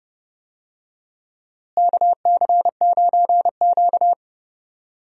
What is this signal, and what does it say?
Morse code, heard as a keyed tone, about 31 wpm, 700 Hz. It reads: KC9Q
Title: USA Call Signs - 4 Characters - Repeated 3x - 30wpm